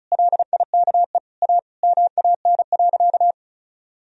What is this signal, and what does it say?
Morse code, heard as a keyed tone, about 36 wpm, 700 Hz. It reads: LIKE A MAN.
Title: Sentences from Top 300 Words and Abbreviations 35wpm